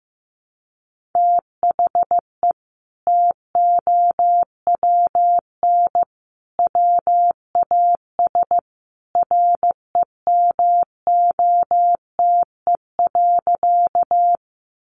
Morse code, heard as a keyed tone, about 15 wpm, 700 Hz. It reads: THE TOWN WAS REMOTE.
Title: Sentences from Top 1300 Words 15wpm